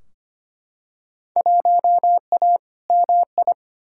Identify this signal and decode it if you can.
Morse code, heard as a keyed tone, about 25 wpm, 700 Hz. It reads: 1A MI
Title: Field Day Exchanges - 25wpm